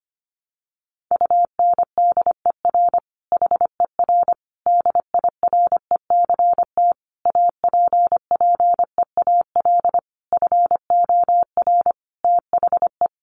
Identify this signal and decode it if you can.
Morse code, heard as a keyed tone, about 25 wpm, 700 Hz. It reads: UNDER HER DIRECT APPEAL FOR THE
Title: Sentences from Top 1300 Words 25wpm